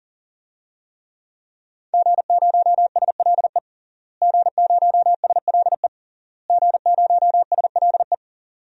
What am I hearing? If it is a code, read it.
Morse code, heard as a keyed tone, about 40 wpm, 700 Hz. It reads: G0SLE G0SLE G0SLE
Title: DXCC-100 Call Signs - 5 Characters - Repeated 3x - 40wpm